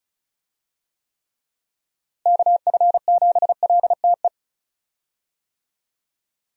Morse code, heard as a keyed tone, about 36 wpm, 700 Hz. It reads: KF7LTE
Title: USA Call Signs - 6 Characters - Repeated 2x - 35wpm